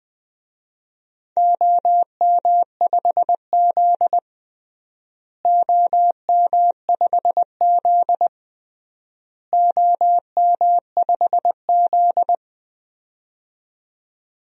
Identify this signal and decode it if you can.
Morse code, heard as a keyed tone, about 20 wpm, 700 Hz. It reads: OM5Z OM5Z OM5Z
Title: DXCC-100 Call Signs - 4 Characters - Repeated 3x - 20wpm